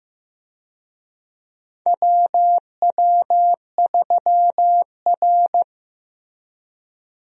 Morse code, heard as a keyed tone, about 15 wpm, 700 Hz. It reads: WW3R
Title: USA Call Signs - 4 Characters - Repeated 2x - 15wpm